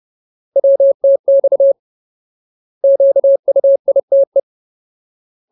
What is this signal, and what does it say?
Morse code, heard as a keyed tone, about 30 wpm, 550 Hz. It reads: WTX QUITE
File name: Ultimate Word List / 30WPM Ultimate